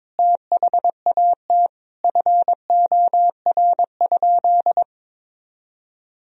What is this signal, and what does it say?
Morse code, heard as a keyed tone, about 22 wpm, 700 Hz. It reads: THAT FOR?
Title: Sentences from Top 1000 Words - Review Part 4 - 22wpm